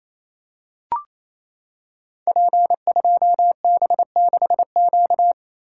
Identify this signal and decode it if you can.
Morse code, heard as a keyed tone, about 27 wpm, 700 Hz. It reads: P2B6Q
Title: Code Groups - 5 Letter:Numbers - 28wpm